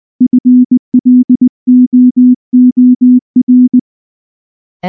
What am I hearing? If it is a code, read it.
Morse code, heard as a keyed tone, about 20 wpm, 250 Hz. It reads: FLOOR
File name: morse / morse-training-20WPM-9